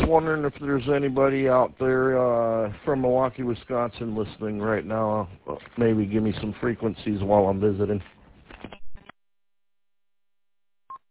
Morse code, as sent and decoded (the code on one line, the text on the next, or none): none